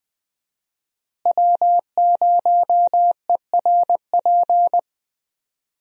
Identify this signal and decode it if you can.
Morse code, heard as a keyed tone, about 20 wpm, 700 Hz. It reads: W0ERP
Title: USA Call Signs - 5 Characters - Repeated 2x - 20wpm